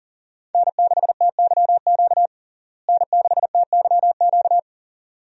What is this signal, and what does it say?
Morse code, heard as a keyed tone, about 40 wpm, 700 Hz. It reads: N6TYQ N6TYQ
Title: USA Call Signs - 5 Characters - Repeated 2x - 40wpm